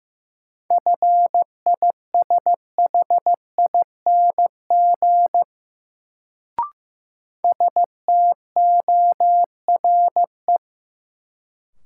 Morse code, heard as a keyed tone, about 15 wpm, 700 Hz. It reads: FISHING STORE
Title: Sentences from Top 1000 Words - Review Part 5 - New Word Drill - 15wpm